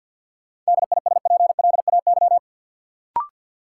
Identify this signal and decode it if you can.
Morse code, heard as a keyed tone, about 53 wpm, 700 Hz. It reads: DISPLAY
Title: Sentences from Top 1200 Words - New Word Drill - 50wpm